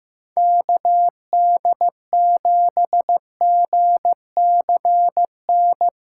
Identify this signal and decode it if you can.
Morse code, heard as a keyed tone, about 15 wpm, 700 Hz. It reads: KD7GCN
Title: USA Callsign DE Callsign 15wpm